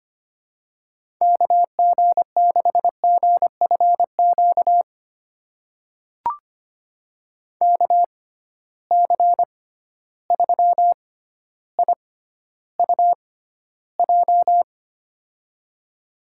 Morse code, heard as a keyed tone, about 25 wpm, spaced as for 13 wpm, 700 Hz. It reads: KG6GFQ KC3IUJ
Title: USA Call Signs - 6 Characters - 6x Character Spacing - 25wpm